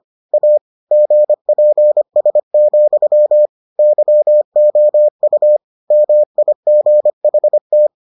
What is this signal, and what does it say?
Morse code, heard as a keyed tone, about 25 wpm, 600 Hz. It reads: A GPS, YOU MIGHT